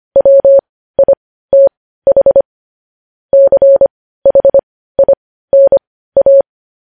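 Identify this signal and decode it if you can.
Morse code, heard as a keyed tone, about 25 wpm, spaced as for 15 wpm, 550 Hz. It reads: WITH CHINA
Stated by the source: Morse code sending of news headlines at 15 wpm.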